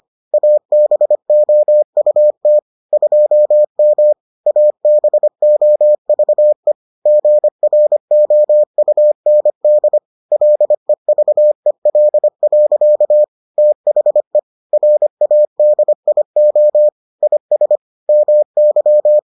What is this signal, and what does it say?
Morse code, heard as a keyed tone, about 25 wpm, 600 Hz. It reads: ABOUT 2M ABOVE GROUND LEVEL. THE RADIO IS MY